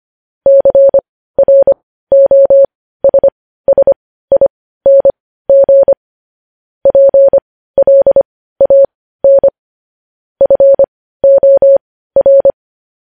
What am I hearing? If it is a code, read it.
Morse code, heard as a keyed tone, about 25 wpm, spaced as for 15 wpm, 550 Hz. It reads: CROSSING PLAN FOR